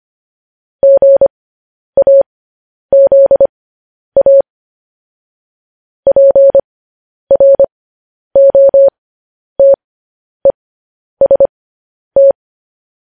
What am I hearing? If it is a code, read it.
Morse code, heard as a keyed tone, about 25 wpm, spaced as for 10 wpm, 550 Hz. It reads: GAZA PROTEST